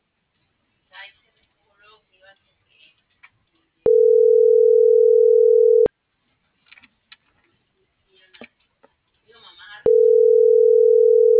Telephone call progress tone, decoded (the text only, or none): ringback tone